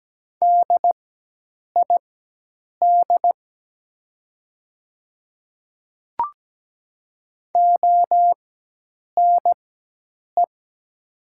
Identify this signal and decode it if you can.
Morse code, heard as a keyed tone, about 17 wpm, spaced as for 8 wpm, 700 Hz. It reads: DID ONE
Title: 4x Character Spacing - Top 100 Words - 17wpm